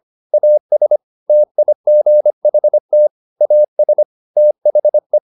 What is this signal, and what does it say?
Morse code, heard as a keyed tone, about 25 wpm, 600 Hz. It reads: AS TIGHT AS THE
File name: the old man and the sea part 1 25wpm